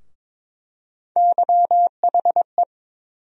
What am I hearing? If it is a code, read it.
Morse code, heard as a keyed tone, about 22 wpm, 700 Hz. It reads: YHE
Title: USA Call Sign Postfixes - 3 Letters 22wpm